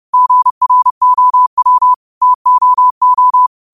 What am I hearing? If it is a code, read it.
Morse code, heard as a keyed tone, about 30 wpm, 1000 Hz. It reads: GROW TOO
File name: qotdfiles / 30wpm